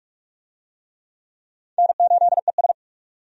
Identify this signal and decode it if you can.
Morse code, heard as a keyed tone, about 46 wpm, 700 Hz. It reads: N8ES
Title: USA Call Signs - 4 Characters - 45wpm